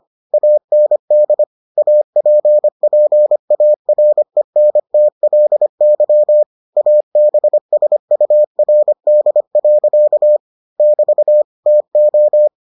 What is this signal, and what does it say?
Morse code, heard as a keyed tone, about 25 wpm, 600 Hz. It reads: AND APPARENTLY ABSURD. <BT> TO